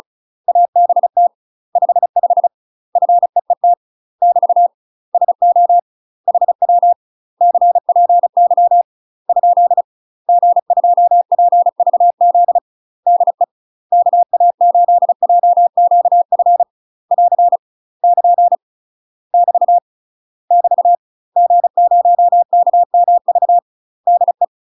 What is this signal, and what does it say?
Morse code, heard as a keyed tone, about 36 wpm, 700 Hz. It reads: ABT 55 FEET <BT> SO HW CPY ? G2PVZ DE KA8JQF <AR> <KN> <BT> <BT> G0KMV DE
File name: UK MASTER QSOs TRAINING / 35WPM / UK MASTER QSOs 0022